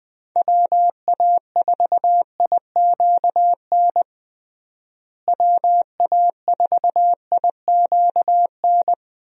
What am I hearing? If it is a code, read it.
Morse code, heard as a keyed tone, about 20 wpm, 700 Hz. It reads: WA4IQN WA4IQN
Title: USA Call Signs - 6 Characters - Repeated 3x - 20wpm